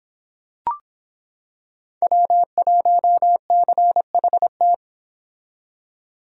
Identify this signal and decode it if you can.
Morse code, heard as a keyed tone, about 26 wpm, 700 Hz. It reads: W1CHT